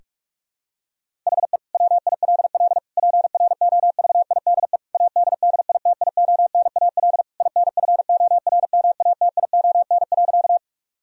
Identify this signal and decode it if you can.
Morse code, heard as a keyed tone, about 46 wpm, 700 Hz. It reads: HE WILL PROVIDE ADDITIONAL INFORMATION.
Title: Sentences from Top 1000 Words - Review Part 4 - 45wpm